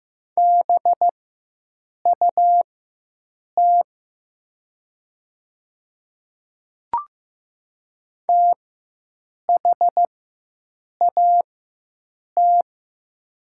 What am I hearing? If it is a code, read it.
Morse code, heard as a keyed tone, about 15 wpm, spaced as for 7 wpm, 700 Hz. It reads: BUT THAT
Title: 4x Character Spacing - Top 100 Words - 15wpm